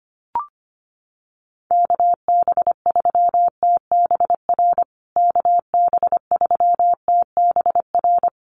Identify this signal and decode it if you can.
Morse code, heard as a keyed tone, about 25 wpm, 700 Hz. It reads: KB3TBR KB3TBR
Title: USA Call Signs Repeated 3x 25wpm